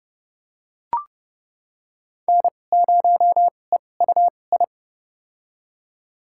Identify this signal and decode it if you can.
Morse code, heard as a keyed tone, about 32 wpm, spaced as for 22 wpm, 700 Hz. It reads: N0EUI
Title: USA Call Signs - 5 Characters - 2x Character Spacing - 30wpm